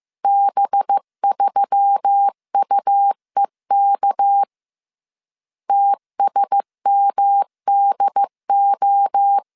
Morse code, heard as a keyed tone, about 15 wpm, 800 Hz. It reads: B3UEK TSMDO
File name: CW / DET / 1 6 1500C S C 14wpm 2951